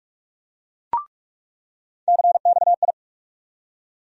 Morse code, heard as a keyed tone, about 46 wpm, 700 Hz. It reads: CXI
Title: USA Call Sign Postfixes - 3 Letters 45wpm